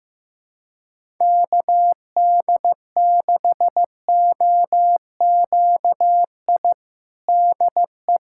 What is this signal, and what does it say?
Morse code, heard as a keyed tone, about 16 wpm, 700 Hz. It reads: KD6OQI DE
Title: USA Callsign DE Callsign - 6 Characters - 15wpm